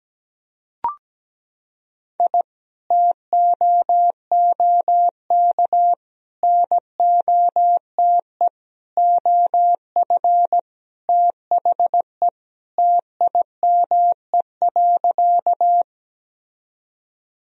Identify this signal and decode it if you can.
Morse code, heard as a keyed tone, about 17 wpm, 700 Hz. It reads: I TOOK NOTE OF THE TIME.
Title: Sentences from Top 1000 Words - Review Part 2 - 17wpm